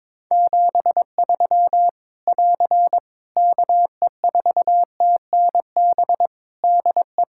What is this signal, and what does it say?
Morse code, heard as a keyed tone, about 22 wpm, 700 Hz. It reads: 73 <AR> KE4TNB DE
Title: K1USN SST QSO Practice 22wpm